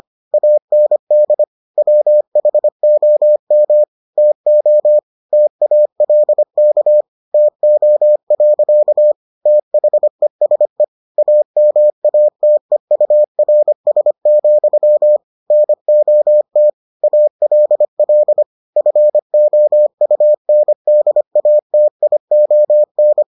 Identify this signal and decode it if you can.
Morse code, heard as a keyed tone, about 25 wpm, 600 Hz. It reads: AND WHOM TO TALK TO. THESE AMATEURS, NOT ALL FOUNDATION